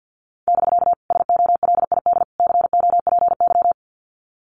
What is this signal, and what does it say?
Morse code, heard as a keyed tone, about 50 wpm, 700 Hz. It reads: <BK> SOLID COPY
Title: Common QSO Phrases Encore 50wpm